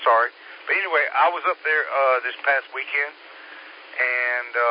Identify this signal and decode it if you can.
Emergency Alert System (SAME) tones, heard as harsh buzzing bursts, with speech. It is heard from 0.6 to 4.3 s.